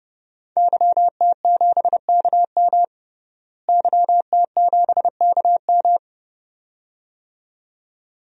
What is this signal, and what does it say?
Morse code, heard as a keyed tone, about 31 wpm, 700 Hz. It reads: YT7KM YT7KM
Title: DXCC-100 Call Signs - 5 Characters - Repeated 2x - 30wpm